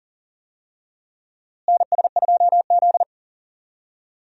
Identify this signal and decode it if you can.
Morse code, heard as a keyed tone, about 40 wpm, 700 Hz. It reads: NS2Z
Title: USA Call Signs - 4 Characters - Repeated 2x - 40wpm